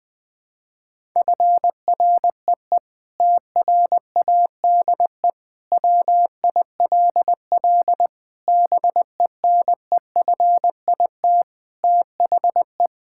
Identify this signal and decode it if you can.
Morse code, heard as a keyed tone, about 20 wpm, 700 Hz. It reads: FREE TRADE WILL BENEFIT THE
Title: Sentences from Top 1000 Words - Review Part 6 - 20wpm